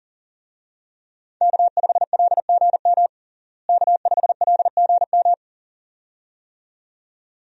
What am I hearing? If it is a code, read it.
Morse code, heard as a keyed tone, about 40 wpm, 700 Hz. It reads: K5LGM K5LGM
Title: USA Call Signs - 5 Characters - Repeated 2x - 40wpm